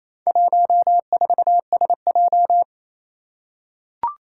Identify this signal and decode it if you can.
Morse code, heard as a keyed tone, about 28 wpm, 700 Hz. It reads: 14SJ
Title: Four Challenging Letter-Number Combinations 28wpm